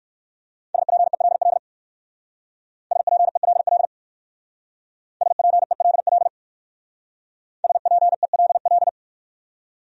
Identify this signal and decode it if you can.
Morse code, heard as a keyed tone, about 58 wpm, 700 Hz. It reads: SPELL SPELL SPELL SPELL